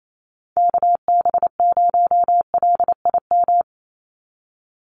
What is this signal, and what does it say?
Morse code, heard as a keyed tone, about 28 wpm, 700 Hz. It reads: KB0LIM
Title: USA Call Signs 28wpm